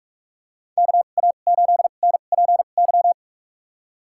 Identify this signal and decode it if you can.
Morse code, heard as a keyed tone, about 46 wpm, spaced as for 32 wpm, 700 Hz. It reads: KA8NPY
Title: USA Call Signs - 6 Characters - 2x Character Spacing - 45wpm